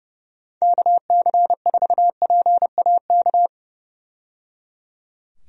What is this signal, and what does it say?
Morse code, heard as a keyed tone, about 30 wpm, 700 Hz. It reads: KC4PAK
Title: USA Call Signs - 6 Characters - 30wpm